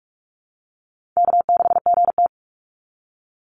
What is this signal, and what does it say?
Morse code, heard as a keyed tone, about 44 wpm, 700 Hz. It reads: K6GT
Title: Easy Call Signs Repeated 2x - 45wpm